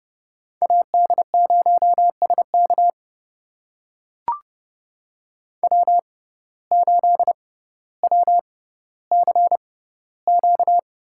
Morse code, heard as a keyed tone, about 30 wpm, spaced as for 15 wpm, 700 Hz. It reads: AD0SK W8WCQ
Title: USA Call Signs - 5 Characters - 6x Character Spacing - 30wpm